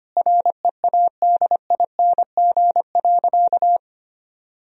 Morse code, heard as a keyed tone, about 25 wpm, 700 Hz. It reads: READING.
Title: Sentences from Top 1000 Words - Review Part 3 - 25wpm